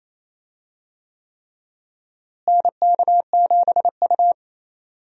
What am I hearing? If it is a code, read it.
Morse code, heard as a keyed tone, about 28 wpm, 700 Hz. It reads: NK7U
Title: USA Call Signs - 4 Characters - Repeated 3x - 28wpm